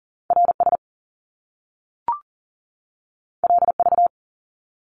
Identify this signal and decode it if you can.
Morse code, heard as a keyed tone, about 40 wpm, 700 Hz. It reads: RS LV
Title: Top Two Letter Combinations in Words 40wpm